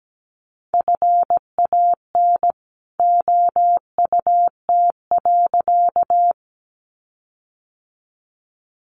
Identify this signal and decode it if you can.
Morse code, heard as a keyed tone, about 17 wpm, 700 Hz. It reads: FAN OUT.
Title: Sentences from Top 1300 Words 17wpm